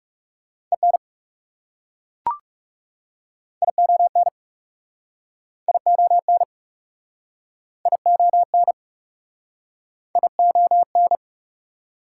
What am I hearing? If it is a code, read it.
Morse code, heard as a keyed tone, about 40 wpm, 690 Hz. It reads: EN ION ION ION ION